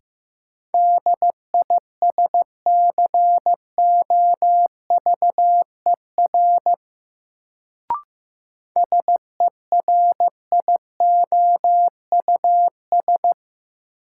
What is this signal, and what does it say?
Morse code, heard as a keyed tone, about 15 wpm, 700 Hz. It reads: DISCOVER SERIOUS
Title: Sentences from Top 1100 Words - New Word Drill - 15wpm